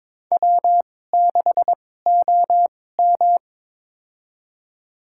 Morse code, heard as a keyed tone, about 22 wpm, spaced as for 16 wpm, 700 Hz. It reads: W6OM
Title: USA Call Signs - 4 Characters - 2x Character Spacing - 22wpm